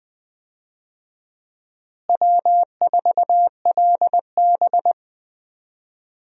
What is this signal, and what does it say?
Morse code, heard as a keyed tone, about 20 wpm, 700 Hz. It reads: W4LB
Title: USA Call Signs - 4 Characters - Repeated 2x - 20wpm